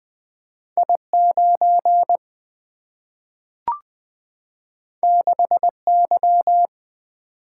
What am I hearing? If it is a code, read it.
Morse code, heard as a keyed tone, about 20 wpm, 700 Hz. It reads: I9 6Y
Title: Single-Letter Single-Number 20wpm